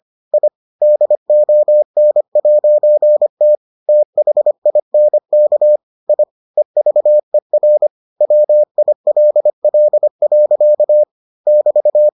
Morse code, heard as a keyed tone, about 25 wpm, 600 Hz. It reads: I DON'T THINK I EVER WILL. <BT>